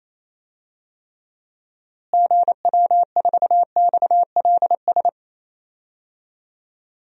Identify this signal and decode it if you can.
Morse code, heard as a keyed tone, about 28 wpm, 700 Hz. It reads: GW4XLS
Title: DXCC-100 Call Signs - 6 Characters - Repeated 3x - 28wpm